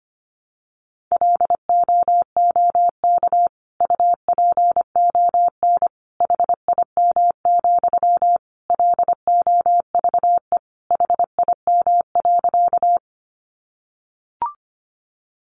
Morse code, heard as a keyed tone, about 25 wpm, 700 Hz. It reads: LOOK UPON HIM, LOVE HIM.
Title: Sentences from Top 1100 Words 25wpm